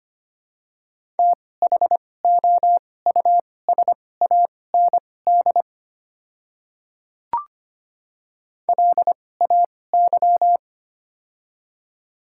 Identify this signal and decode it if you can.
Morse code, heard as a keyed tone, about 25 wpm, spaced as for 18 wpm, 700 Hz. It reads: THOUSAND LAY
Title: 2x Character Spacing - Top 500 Words - 25wpm